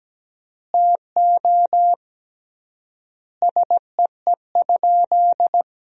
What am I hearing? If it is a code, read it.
Morse code, heard as a keyed tone, about 17 wpm, 700 Hz. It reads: TO SEE?
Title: 3x Word-Spacing - Sentences from Top 100 Words - 17wpm